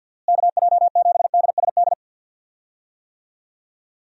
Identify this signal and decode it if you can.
Morse code, heard as a keyed tone, about 52 wpm, 700 Hz. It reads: KJ7DSD
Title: USA Call Signs - 6 Characters - Repeated 3x - 50wpm